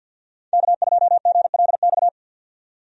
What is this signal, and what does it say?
Morse code, heard as a keyed tone, about 52 wpm, 700 Hz. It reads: K2GLX
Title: USA Call Signs - 5 Characters - 50wpm